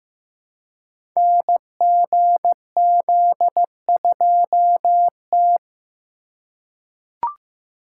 Morse code, heard as a keyed tone, about 15 wpm, 700 Hz. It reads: NGZ2T
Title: Code Groups - 5 Letter:Numbers - 15wpm